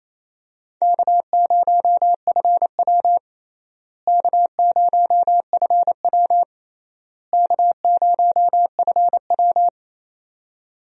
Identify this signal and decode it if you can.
Morse code, heard as a keyed tone, about 28 wpm, 700 Hz. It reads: K0FW K0FW K0FW
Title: USA Call Signs - 4 Characters - Repeated 3x - 28wpm